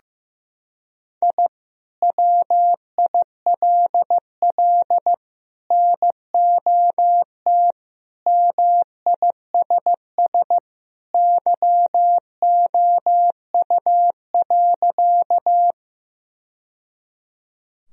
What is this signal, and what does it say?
Morse code, heard as a keyed tone, about 15 wpm, 700 Hz. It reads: I WILL NOT MISS YOU.
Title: Sentences from Top 1000 Words - Review Part 3 - 15wpm